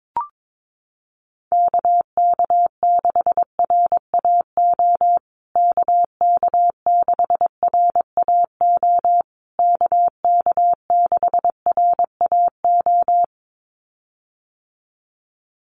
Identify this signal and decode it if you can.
Morse code, heard as a keyed tone, about 22 wpm, 700 Hz. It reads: KK6RAO KK6RAO KK6RAO
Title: USA Call Signs Repeated 3x 22wpm